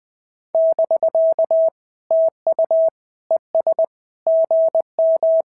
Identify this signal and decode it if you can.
Morse code, heard as a keyed tone, about 19 wpm, 650 Hz. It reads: <BK> TU ES GM
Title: SOTA QSO Practice 20wpm